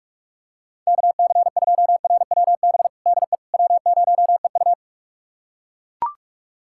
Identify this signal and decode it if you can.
Morse code, heard as a keyed tone, about 42 wpm, 690 Hz. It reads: KC2RWB DE W0EU
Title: USA Callsign DE Callsign 45wpm